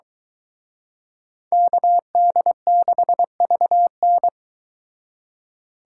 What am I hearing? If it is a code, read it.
Morse code, heard as a keyed tone, about 23 wpm, 700 Hz. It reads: KD6VN